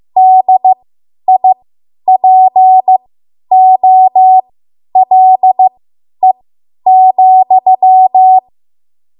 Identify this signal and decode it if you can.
Morse code, heard as a keyed tone, about 15 wpm, spaced as for 10 wpm, 750 Hz. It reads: DIPOLE,
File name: Archive / 10 WPM / 10WPM